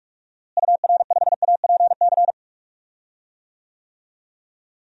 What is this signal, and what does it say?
Morse code, heard as a keyed tone, about 41 wpm, 700 Hz. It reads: UR5APC
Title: DXCC-100 Call Signs - 6 Characters - Repeated 2x - 45wpm